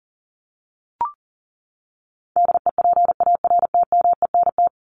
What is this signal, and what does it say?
Morse code, heard as a keyed tone, about 40 wpm, 700 Hz. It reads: DEPARTMENT
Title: Top 751-1000 Words 40wpm